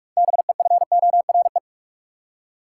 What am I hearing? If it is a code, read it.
Morse code, heard as a keyed tone, about 43 wpm, 700 Hz. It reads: BEFORE
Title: Six Letter Words - 45wpm